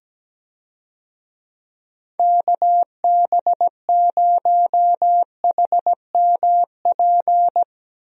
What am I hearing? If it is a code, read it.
Morse code, heard as a keyed tone, about 17 wpm, 700 Hz. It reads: KB0HMP
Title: USA Call Signs - 6 Characters - 17wpm